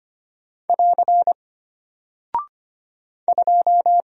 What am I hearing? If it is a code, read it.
Morse code, heard as a keyed tone, about 25 wpm, 700 Hz. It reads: <AR> 2
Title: Single Character (Letter:Number:Punctuation:Prosign) Alt 25wpm